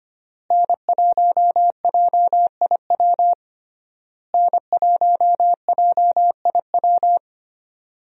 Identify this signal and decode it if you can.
Morse code, heard as a keyed tone, about 25 wpm, 700 Hz. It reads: N1JIW N1JIW
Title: USA Call Signs - 5 Characters - Repeated 3x - 25wpm